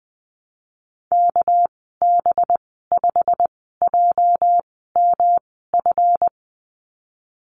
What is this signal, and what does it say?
Morse code, heard as a keyed tone, about 20 wpm, spaced as for 15 wpm, 700 Hz. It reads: KB5JMF
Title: 219 Lesson 35g - Number 0 - Call Signs 20wpm